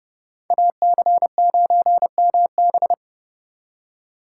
Morse code, heard as a keyed tone, about 31 wpm, 700 Hz. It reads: AC9MB